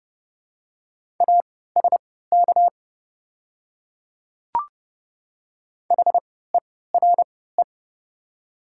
Morse code, heard as a keyed tone, about 30 wpm, spaced as for 17 wpm, 700 Hz. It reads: ASK HERE
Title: 3x Character Spacing - Top 200 Words - 30wpm